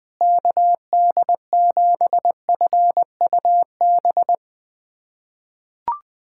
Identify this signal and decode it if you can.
Morse code, heard as a keyed tone, about 20 wpm, 700 Hz. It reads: KD7FUB
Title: USA Callsign DE Callsign - 6 Characters - 20wpm